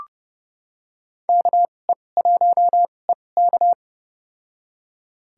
Morse code, heard as a keyed tone, about 32 wpm, spaced as for 22 wpm, 700 Hz. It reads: KE1EK
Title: USA Call Signs - 5 Characters - 2x Character Spacing - 30wpm